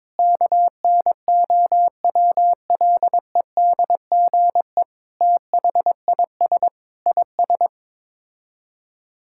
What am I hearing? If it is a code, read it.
Morse code, heard as a keyed tone, about 22 wpm, 700 Hz. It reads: KNOWLEDGE THIS IS
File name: practice / 22wpm / Sets of 8 Words - 22wpm